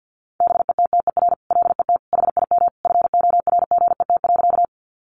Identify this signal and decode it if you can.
Morse code, heard as a keyed tone, about 50 wpm, 700 Hz. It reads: BETTER LET HIM FORGET.
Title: Sentences from Top 1200 Words 50wpm